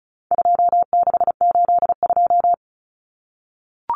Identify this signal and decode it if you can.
Morse code, heard as a keyed tone, about 34 wpm, 700 Hz. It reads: 2682
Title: Common Cut Numbers - 4-Digit 35wpm